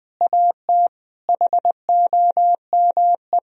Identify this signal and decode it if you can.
Morse code, heard as a keyed tone, about 20 wpm, 700 Hz. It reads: AT HOME
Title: Sentences from Top 1000 Words - Review Part 3 - 20wpm